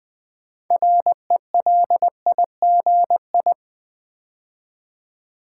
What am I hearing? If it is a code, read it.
Morse code, heard as a keyed tone, about 20 wpm, 700 Hz. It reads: RELIGI
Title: Build A Word - 3 Letters At A Time - 8-Letter Words - 20wpm